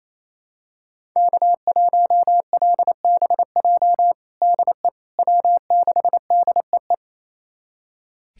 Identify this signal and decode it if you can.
Morse code, heard as a keyed tone, about 28 wpm, 700 Hz. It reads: K1LBJ DE W6DEE
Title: USA Callsign DE Callsign - 5 Characters - 28wpm